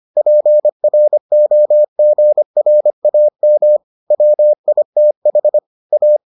Morse code, heard as a keyed tone, about 25 wpm, 600 Hz. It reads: PROGRAM WITH A